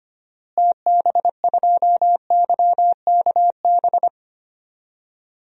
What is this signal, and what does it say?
Morse code, heard as a keyed tone, about 25 wpm, 700 Hz. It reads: TB2YKB
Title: DXCC-100 Call Signs - 6 Characters - Repeated 2x - 25wpm